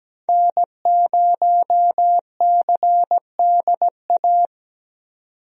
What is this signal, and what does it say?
Morse code, heard as a keyed tone, about 17 wpm, 700 Hz. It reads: N0CDA
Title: USA Call Signs - 5 Characters - Repeated 3x - 17wpm